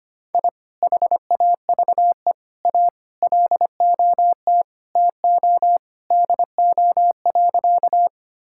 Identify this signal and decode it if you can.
Morse code, heard as a keyed tone, about 25 wpm, 700 Hz. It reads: I HAVE A LOT TO DO.
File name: practice / 25wpm / Sentences with 2-Word Phrases - 25wpm